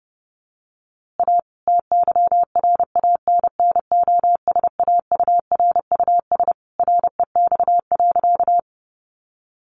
Morse code, heard as a keyed tone, about 30 wpm, 700 Hz. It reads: A TYRANNOSAURUS REX.
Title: A Sound Of Thunder 30wpm